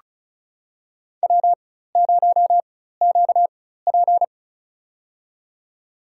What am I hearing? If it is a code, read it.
Morse code, heard as a keyed tone, about 36 wpm, spaced as for 16 wpm, 700 Hz. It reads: W0QP
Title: USA Call Signs - 4 Characters - 4x Character Spacing - 35wpm